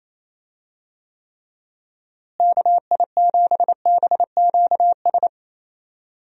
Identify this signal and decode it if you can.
Morse code, heard as a keyed tone, about 28 wpm, 700 Hz. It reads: KI7BQS
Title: USA Call Signs - 6 Characters - Repeated 3x - 28wpm